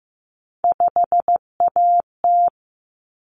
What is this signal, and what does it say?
Morse code, heard as a keyed tone, about 15 wpm, 700 Hz. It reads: 5AT
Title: Common Cut Numbers - 3-Digit 15wpm